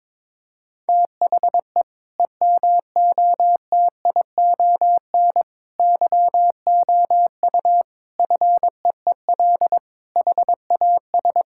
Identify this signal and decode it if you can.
Morse code, heard as a keyed tone, about 22 wpm, 700 Hz. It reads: THE EMOTION YOU FEEL HAS
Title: Sentences from Top 1600 Words 22wpm